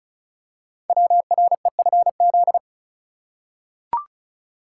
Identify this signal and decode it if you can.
Morse code, heard as a keyed tone, about 36 wpm, 700 Hz. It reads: WREFZ
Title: Code Groups - 5 Letter:Numbers - 35wpm